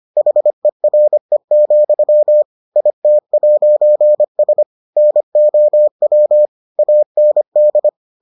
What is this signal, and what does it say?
Morse code, heard as a keyed tone, about 25 wpm, 600 Hz. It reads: HERE, IT'S NOW AND